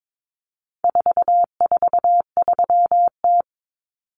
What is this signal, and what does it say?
Morse code, heard as a keyed tone, about 22 wpm, 700 Hz. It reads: 443T
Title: Common Cut Numbers - 4-Digit 22wpm